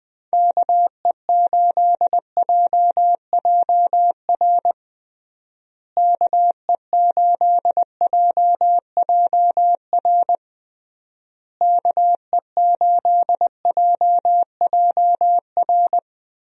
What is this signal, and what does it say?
Morse code, heard as a keyed tone, about 21 wpm, 700 Hz. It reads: KE8JJR KE8JJR KE8JJR
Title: USA Call Signs - 6 Characters - Repeated 3x - 20wpm